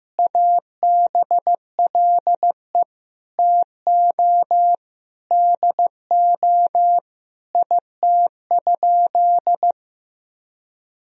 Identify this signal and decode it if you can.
Morse code, heard as a keyed tone, about 16 wpm, 700 Hz. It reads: ABLE TO DO IT?
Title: Sentences from Top 1000 Words - Review Part 2 - 15wpm